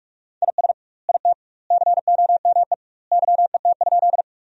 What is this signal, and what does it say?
Morse code, heard as a keyed tone, about 46 wpm, 700 Hz. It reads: IS IT COME YET?
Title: 2x Word-Spacing - Sentences from Top 500 Words - 45wpm